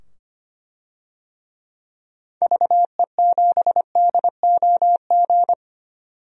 Morse code, heard as a keyed tone, about 25 wpm, 700 Hz. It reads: VE7DOG
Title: DXCC-100 Call Signs - 6 Characters - 25wpm